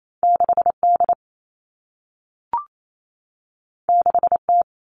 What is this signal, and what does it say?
Morse code, heard as a keyed tone, about 28 wpm, 700 Hz. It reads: 6D 6T
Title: All Cut Numbers - 2-Digit 28wpm